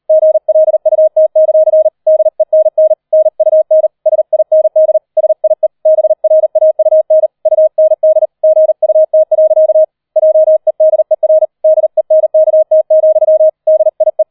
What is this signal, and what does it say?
Morse code, heard as a keyed tone, about 38 wpm, 630 Hz. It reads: GLUT; DENN NUN SIND SIE BRAUN UND GUT. JEDER DENKT, DIE